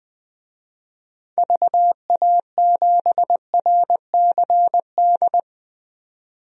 Morse code, heard as a keyed tone, about 20 wpm, 700 Hz. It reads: VA7RCD
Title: DXCC-100 Call Signs - 6 Characters - 20wpm